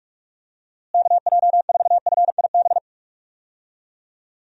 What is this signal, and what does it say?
Morse code, heard as a keyed tone, about 44 wpm, 700 Hz. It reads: KJ4FIB
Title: USA Call Signs - 6 Characters - Repeated 3x - 45wpm